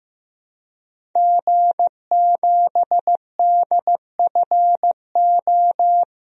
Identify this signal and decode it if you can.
Morse code, heard as a keyed tone, about 15 wpm, 700 Hz. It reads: G7DFO
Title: DXCC-100 Call Signs - 5 Characters - Repeated 2x - 15wpm